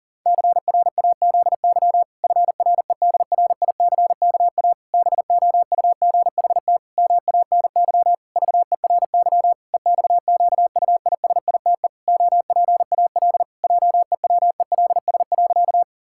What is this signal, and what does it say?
Morse code, heard as a keyed tone, about 37 wpm, 700 Hz. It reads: CRAZY FREDRICKA BOUGHT MANY VERY EXQUISITE OPAL JEWELS.
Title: Pangram Challenge v2 - 40wpm